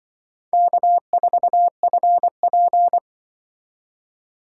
Morse code, heard as a keyed tone, about 25 wpm, 690 Hz. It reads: K4FP
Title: Speed Racing - USA 1x2 Call Signs 20wpm